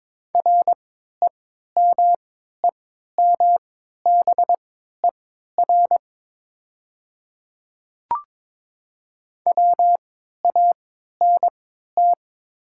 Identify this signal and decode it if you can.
Morse code, heard as a keyed tone, about 22 wpm, spaced as for 12 wpm, 700 Hz. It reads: REMEMBER WANT
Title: 3x Character Spacing - Top 500 Words - 22wpm